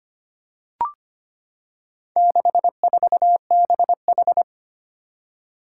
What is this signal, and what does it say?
Morse code, heard as a keyed tone, about 25 wpm, 700 Hz. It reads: 64BH
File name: practice / 25wpm / Four Letter-Number Combinations - 25wpm